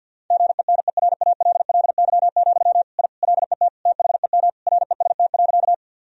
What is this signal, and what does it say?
Morse code, heard as a keyed tone, about 50 wpm, 700 Hz. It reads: GENERALLY, I LET THEM REST.
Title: Sentences from Top 1000 Words - Review Part 4 - 50wpm